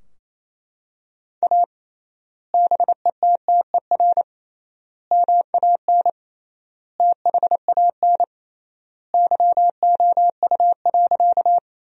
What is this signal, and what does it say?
Morse code, heard as a keyed tone, about 28 wpm, 700 Hz. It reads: A BETTER MAN THAN YOU.
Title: 3x Word-Spacing - Sentences from Top 400 Words - 28wpm